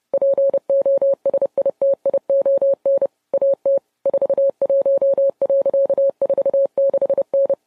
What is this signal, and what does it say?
Morse code, heard as a keyed tone, about 30 wpm, 550 Hz. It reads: POSITION AT 41.46N